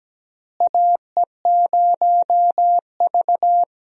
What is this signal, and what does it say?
Morse code, heard as a keyed tone, about 17 wpm, 700 Hz. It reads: AE0V
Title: USA Call Signs - 4 Characters - Repeated 3x - 17wpm